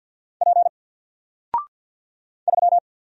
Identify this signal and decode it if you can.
Morse code, heard as a keyed tone, about 50 wpm, 700 Hz. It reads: P 3
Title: Single Character (Letter:Number:Punctuation:Prosign) Alt 50wpm